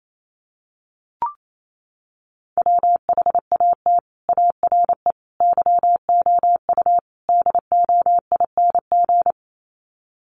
Morse code, heard as a keyed tone, about 28 wpm, 700 Hz. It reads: WHAT ARE YOU DOING
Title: Sets of 4 Words 28wpm